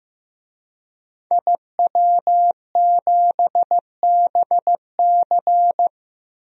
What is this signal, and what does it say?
Morse code, heard as a keyed tone, about 15 wpm, 700 Hz. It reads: IW7BC
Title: DXCC-100 Call Signs - 5 Characters - Repeated 3x - 15wpm